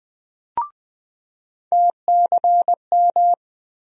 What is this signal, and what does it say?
Morse code, heard as a keyed tone, about 20 wpm, 700 Hz. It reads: TCM
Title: USA Call Sign Postfixes - 3 Letters 20wpm